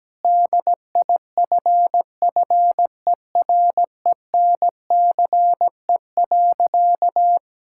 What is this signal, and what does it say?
Morse code, heard as a keyed tone, about 17 wpm, 700 Hz. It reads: DIFFERENCE.
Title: Sentences from Top 1000 Words - Review Part 5 - 17wpm